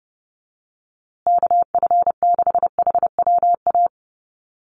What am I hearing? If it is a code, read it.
Morse code, heard as a keyed tone, about 30 wpm, 700 Hz. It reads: KF6HWA
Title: USA Call Signs Repeated 2x 30wpm